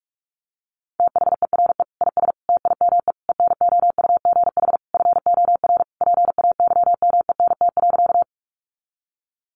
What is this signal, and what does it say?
Morse code, heard as a keyed tone, about 44 wpm, 700 Hz. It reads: THERE IS TIME ENOUGH FOR PAYMENT.
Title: Sentences from Top 1200 Words 45wpm